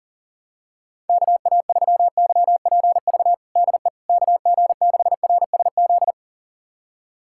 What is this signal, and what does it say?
Morse code, heard as a keyed tone, about 40 wpm, 700 Hz. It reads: KA3YPV DE KG6RSZ
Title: USA Callsign DE Callsign 40wpm